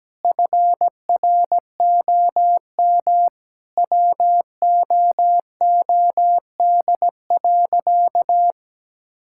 Morse code, heard as a keyed tone, about 17 wpm, 700 Hz. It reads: FROM WOOD.
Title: Sentences from Top 1000 Words - Review Part 6 - 17wpm